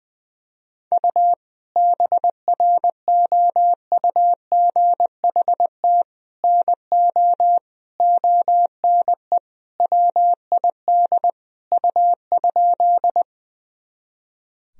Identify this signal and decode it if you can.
Morse code, heard as a keyed tone, about 20 wpm, 700 Hz. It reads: U BROUGHT NO ONE WID U?
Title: Sentences from Top 500 Words and Abbreviations 20wpm